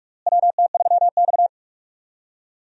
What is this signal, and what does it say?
Morse code, heard as a keyed tone, about 46 wpm, 700 Hz. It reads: WT3X
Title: USA Call Signs - 4 Characters - Repeated 2x - 45wpm